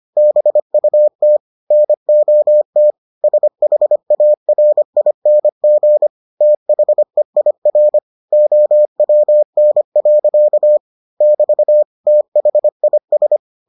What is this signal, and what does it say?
Morse code, heard as a keyed tone, about 25 wpm, 600 Hz. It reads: BUT NOT SHARING THEIR OWN. <BT> THIS